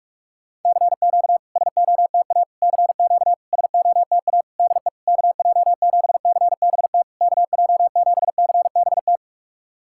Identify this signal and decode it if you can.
Morse code, heard as a keyed tone, about 46 wpm, 690 Hz. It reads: CQ SOTA CQ SOTA DE KJ7CBT KJ7CBT
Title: SOTA QSO Practice 45wpm